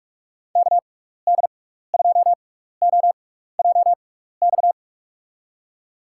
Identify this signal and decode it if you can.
Morse code, heard as a keyed tone, about 44 wpm, spaced as for 15 wpm, 700 Hz. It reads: KD2OJX
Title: USA Call Signs - 6 Characters - 6x Character Spacing - 45wpm